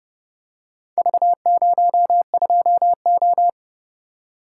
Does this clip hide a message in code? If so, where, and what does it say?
Yes: Morse code from 1.0 to 3.5 s, heard as a keyed tone, about 29 wpm, 700 Hz. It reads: V02O